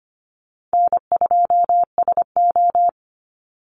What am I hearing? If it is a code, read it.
Morse code, heard as a keyed tone, about 25 wpm, 700 Hz. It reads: N2SO
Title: Easy Call Signs Repeated 3x - 25wpm